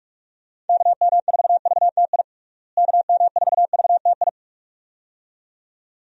Morse code, heard as a keyed tone, about 45 wpm, 700 Hz. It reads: KM4VTI KM4VTI
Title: USA Call Signs - 6 Characters - Repeated 2x - 45wpm